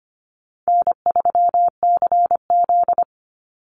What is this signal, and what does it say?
Morse code, heard as a keyed tone, about 25 wpm, 700 Hz. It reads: N3CZ
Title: Easy Call Signs Repeated 3x - 25wpm